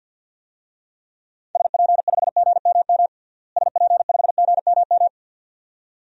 Morse code, heard as a keyed tone, about 51 wpm, 700 Hz. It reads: SP5GMM SP5GMM
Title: DXCC-100 Call Signs - 6 Characters - Repeated 2x - 50wpm